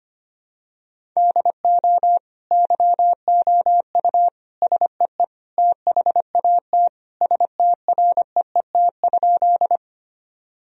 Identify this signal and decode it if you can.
Morse code, heard as a keyed tone, about 25 wpm, 700 Hz. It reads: DO YOU SEE THAT STREET?
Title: Sentences from Top 1000 Words - Review Part 3 - 25wpm